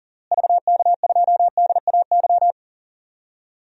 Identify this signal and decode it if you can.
Morse code, heard as a keyed tone, about 40 wpm, 700 Hz. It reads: VK2DAY
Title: DXCC-100 Call Signs - 6 Characters - Repeated 3x - 40wpm